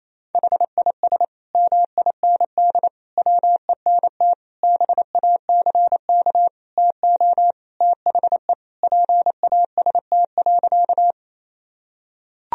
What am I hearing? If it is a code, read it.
Morse code, heard as a keyed tone, about 28 wpm, 700 Hz. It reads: HIS MIND WENT BACK TO THE PAST.
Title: Sentences from Top 1000 Words - Review Part 3 - 28wpm